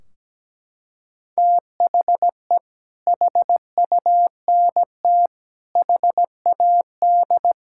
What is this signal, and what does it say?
Morse code, heard as a keyed tone, about 17 wpm, 700 Hz. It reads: THE HUNT HAD
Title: Sentences from Top 1600 Words 17wpm